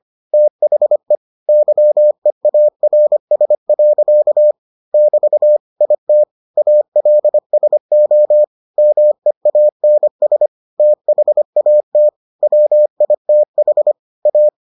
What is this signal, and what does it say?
Morse code, heard as a keyed tone, about 25 wpm, 600 Hz. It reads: THE YEARS. <BT> IT ALSO MEANS THAT WITH A